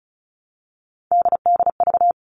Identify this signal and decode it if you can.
Morse code, heard as a keyed tone, about 34 wpm, 700 Hz. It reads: DDV
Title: All Cut Numbers - 3-Digit 35wpm